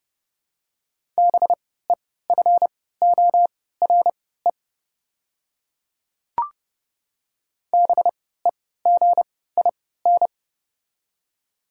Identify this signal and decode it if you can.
Morse code, heard as a keyed tone, about 30 wpm, spaced as for 17 wpm, 700 Hz. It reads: BEFORE BEGIN
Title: 3x Character Spacing - Top 300 Words - 30wpm